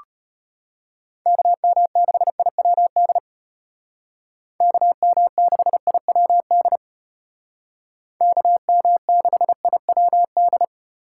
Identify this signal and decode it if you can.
Morse code, heard as a keyed tone, about 30 wpm, 700 Hz. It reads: KM6IWD KM6IWD KM6IWD
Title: Speed Racing - USA 2x3 Call Signs 25wpm